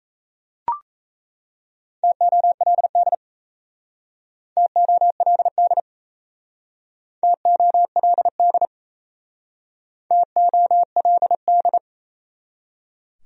Morse code, heard as a keyed tone, about 35 wpm, 700 Hz. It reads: TOLD TOLD TOLD TOLD